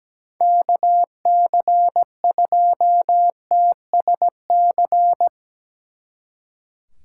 Morse code, heard as a keyed tone, about 19 wpm, 700 Hz. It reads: KC2TSC
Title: Speed Racing - USA Call Signs 17wpm